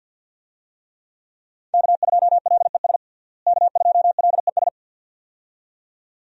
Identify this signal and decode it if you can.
Morse code, heard as a keyed tone, about 50 wpm, 700 Hz. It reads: K2LES K2LES
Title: USA Call Signs - 5 Characters - Repeated 2x - 50wpm